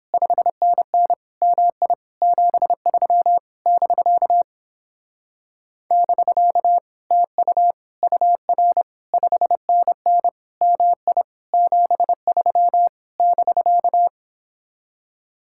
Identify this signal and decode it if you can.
Morse code, heard as a keyed tone, about 29 wpm, 700 Hz. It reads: 5NN MI 73 <BK> <BK> TU UR 5NN MI 73 <BK>